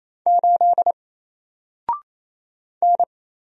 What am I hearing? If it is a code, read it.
Morse code, heard as a keyed tone, about 28 wpm, 700 Hz. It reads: 8 N
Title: Single Letter-Number 28wpm